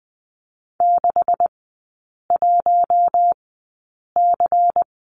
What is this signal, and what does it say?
Morse code, heard as a keyed tone, about 20 wpm, spaced as for 8 wpm, 700 Hz. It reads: 61C
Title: 166 Lesson 27e - Number 7 - Three Characters 20wpm